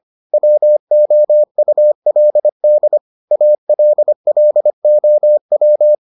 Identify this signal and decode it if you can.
Morse code, heard as a keyed tone, about 25 wpm, 600 Hz. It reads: WOULD ALLOW